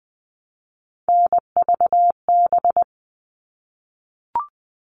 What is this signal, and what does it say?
Morse code, heard as a keyed tone, about 20 wpm, 700 Hz. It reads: NVB
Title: All Cut Numbers - 3-Digit 20wpm